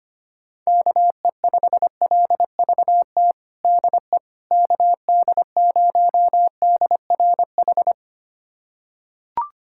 Morse code, heard as a keyed tone, about 25 wpm, 700 Hz. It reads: KE5LVT DE KD0DRH
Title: USA Callsign DE Callsign - 6 Characters - 25wpm